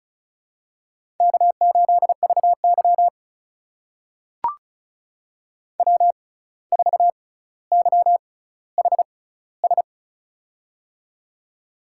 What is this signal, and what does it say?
Morse code, heard as a keyed tone, about 36 wpm, spaced as for 17 wpm, 700 Hz. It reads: K8VY W4YHS
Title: USA Call Signs - 6x Character Spacing - 35wpm